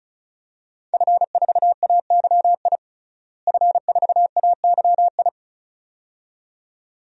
Morse code, heard as a keyed tone, about 36 wpm, 700 Hz. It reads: F4AYI F4AYI
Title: DXCC-100 Call Signs - 5 Characters - Repeated 2x - 35wpm